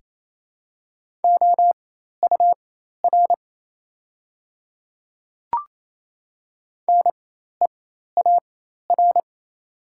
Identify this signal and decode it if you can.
Morse code, heard as a keyed tone, about 27 wpm, spaced as for 13 wpm, 690 Hz. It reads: OUR NEAR
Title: 4x Character Spacing - Top 200 Words - 28wpm